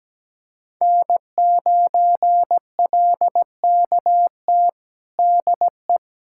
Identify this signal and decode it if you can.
Morse code, heard as a keyed tone, about 17 wpm, 700 Hz. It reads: N9LKT DE
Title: USA Callsign DE Callsign - 5 Characters - 17wpm